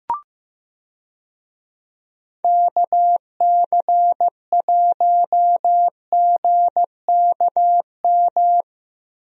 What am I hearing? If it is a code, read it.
Morse code, heard as a keyed tone, about 15 wpm, 700 Hz. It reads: KC1GKM
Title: USA Call Signs - 6 Characters - Repeated 2x - 15wpm